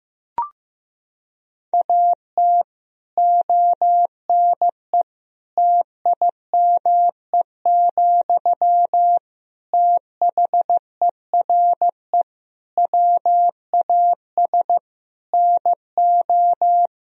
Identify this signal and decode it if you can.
Morse code, heard as a keyed tone, about 16 wpm, 700 Hz. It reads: AT ONE TIME, THERE WAS NO
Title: Sentences from Top 1000 Words - Review Part 1 - 15wpm